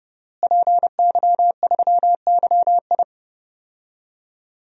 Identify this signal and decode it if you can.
Morse code, heard as a keyed tone, about 32 wpm, 700 Hz. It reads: PY3YI
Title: DXCC-100 Call Signs - 5 Characters - Repeated 3x - 30wpm